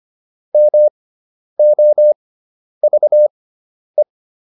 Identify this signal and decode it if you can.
Morse code, heard as a keyed tone, about 25 wpm, spaced as for 10 wpm, 600 Hz. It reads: MOVE